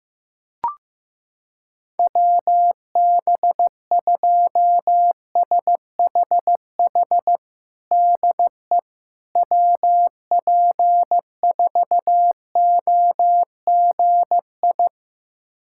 Morse code, heard as a keyed tone, about 15 wpm, 700 Hz. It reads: WB2SHH DE WP4OGI
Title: USA Callsign DE Callsign - 6 Characters - 15wpm